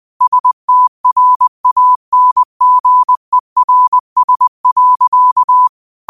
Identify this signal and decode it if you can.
Morse code, heard as a keyed tone, about 20 wpm, 1000 Hz. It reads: STRANGERS.